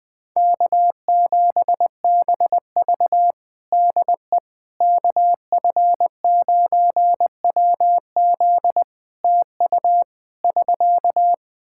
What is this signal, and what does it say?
Morse code, heard as a keyed tone, about 20 wpm, 700 Hz. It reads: K7BV DE KF9WZ TU <SK>
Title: K1USN SST QSO Practice 20wpm